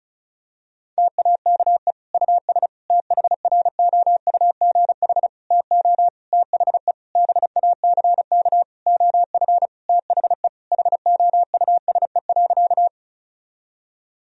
Morse code, heard as a keyed tone, about 36 wpm, 700 Hz. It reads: TAKE US THROUGH TO THE BACK OF THE HOUSE.
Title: Sentences from Top 1000 Words - Review Part 1 - 35wpm